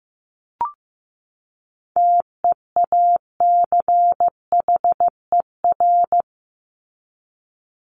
Morse code, heard as a keyed tone, about 15 wpm, 700 Hz. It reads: TEACHER
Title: Top 751-1000 Words 15wpm